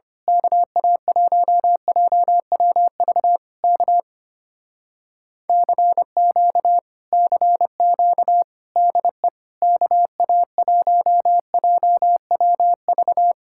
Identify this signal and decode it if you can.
Morse code, heard as a keyed tone, about 24 wpm, 700 Hz. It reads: KA1JWV K CQ CQ DE KA1JWV
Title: Speed-Racing Common QSO Phrases Encore - 25wpm